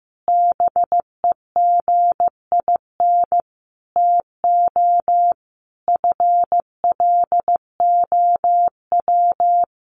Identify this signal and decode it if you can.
Morse code, heard as a keyed tone, about 15 wpm, 700 Hz. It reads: BEGIN TO FLOW
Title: Sentences from Top 1200 Words 15wpm